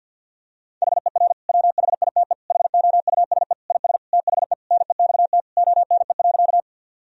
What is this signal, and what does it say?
Morse code, heard as a keyed tone, about 47 wpm, 700 Hz. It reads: HER WHITE HOUSE IS THE NEXT ONE.
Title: Sentences from Top 1000 Words - Review Part 1 - 50wpm